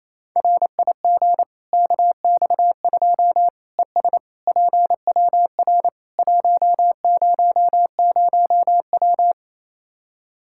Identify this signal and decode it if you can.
Morse code, heard as a keyed tone, about 28 wpm, 700 Hz. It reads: RIG KX2 ES PWR 100W
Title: Practice QSO - Exchange 2 - Info - 28wpm